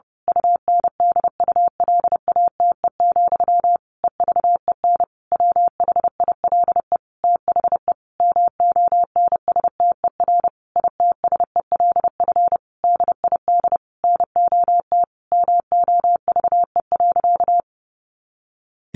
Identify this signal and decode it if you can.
Morse code, heard as a keyed tone, about 30 wpm, 700 Hz. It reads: UNDULATE, EVEN WHILE THE MONSTER ITSELF DID NOT MOVE.
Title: A Sound Of Thunder 30wpm